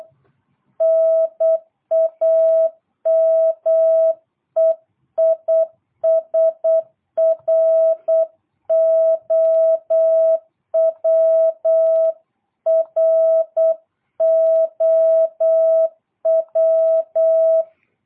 Morse code, heard as a keyed tone, about 8 wpm, 660 Hz. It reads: NAME IS ROW ROW
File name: jh8cbh / 2022cw